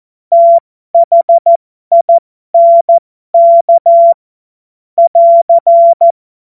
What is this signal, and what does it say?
Morse code, heard as a keyed tone, about 14 wpm, spaced as for 12 wpm, 680 Hz. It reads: THINK <AR>
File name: Morsekurs / Witze